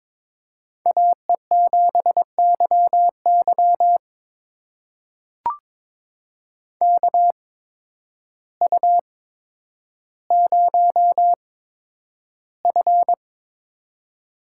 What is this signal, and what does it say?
Morse code, heard as a keyed tone, about 22 wpm, spaced as for 10 wpm, 700 Hz. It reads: AE7YY KU0F
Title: USA Call Signs - 8x Character Spacing - 22wpm